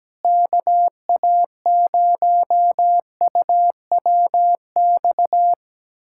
Morse code, heard as a keyed tone, about 17 wpm, 700 Hz. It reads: KA0UWX
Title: USA Callsign DE Callsign - 6 Characters - 17wpm